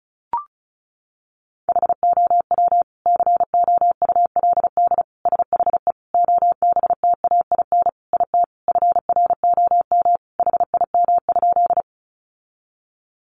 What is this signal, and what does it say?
Morse code, heard as a keyed tone, about 34 wpm, 700 Hz. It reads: HOW COULD SHE OBTAIN IT FROM HIM?
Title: Sentences from Top 1300 Words 35wpm